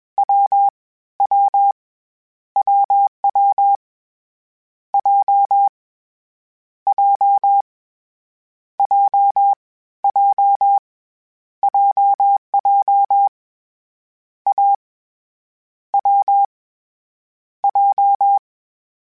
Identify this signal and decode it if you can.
Morse code, heard as a keyed tone, about 21 wpm, 800 Hz. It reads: W W WW J J J J JJ A W J
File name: Difficult Combos / Difficult Combos-mp3s / AWJ ShortRun@20